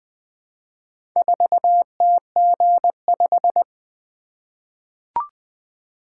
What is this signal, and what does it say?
Morse code, heard as a keyed tone, about 20 wpm, 700 Hz. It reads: 4TG5
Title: Four Letter-Number Combinations - 20wpm